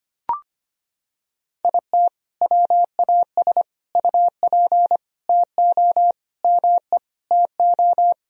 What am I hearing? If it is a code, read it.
Morse code, heard as a keyed tone, about 25 wpm, 700 Hz. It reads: IT WAS UP TO ME TO
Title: Sentences from Top 1000 Words - Review Part 2 - 25wpm